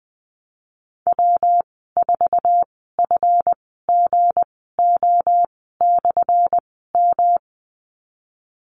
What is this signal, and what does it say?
Morse code, heard as a keyed tone, about 20 wpm, spaced as for 14 wpm, 700 Hz. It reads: W4FGO/M
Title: 179 Lesson 29g - Slash - Call Signs 20wpm